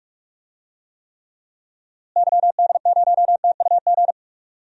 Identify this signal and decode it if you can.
Morse code, heard as a keyed tone, about 45 wpm, 700 Hz. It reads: YD0TUG
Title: DXCC-100 Call Signs - 6 Characters - Repeated 2x - 45wpm